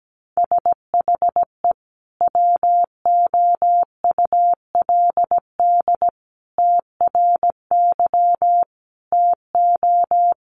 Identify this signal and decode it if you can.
Morse code, heard as a keyed tone, about 17 wpm, 700 Hz. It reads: SHE WOULD TRY TO
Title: Sentences from Top 600 Words 17wpm